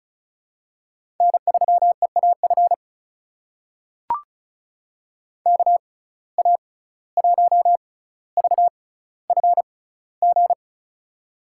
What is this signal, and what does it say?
Morse code, heard as a keyed tone, about 36 wpm, spaced as for 17 wpm, 700 Hz. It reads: N3EAF KA1VFG
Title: USA Call Signs - 6x Character Spacing - 35wpm